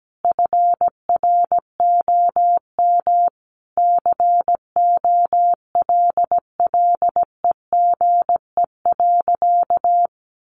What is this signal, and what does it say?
Morse code, heard as a keyed tone, about 17 wpm, 700 Hz. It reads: FROM COLLEGE.
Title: Sentences from Top 1300 Words 17wpm